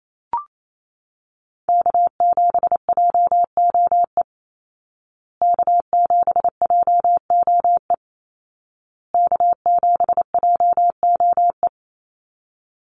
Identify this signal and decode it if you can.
Morse code, heard as a keyed tone, about 28 wpm, 700 Hz. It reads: K7JOE K7JOE K7JOE
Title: Call Signs Repeated 2x - 28wpm